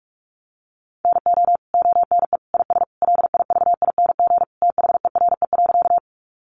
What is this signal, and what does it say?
Morse code, heard as a keyed tone, about 44 wpm, 700 Hz. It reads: NO ONE IS LIVING THERE.
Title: Sentences from Top 1100 Words 45wpm